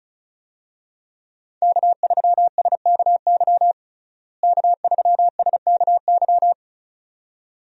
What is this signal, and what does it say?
Morse code, heard as a keyed tone, about 34 wpm, 700 Hz. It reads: K3SKY K3SKY
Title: USA Call Signs - 5 Characters - Repeated 2x - 35wpm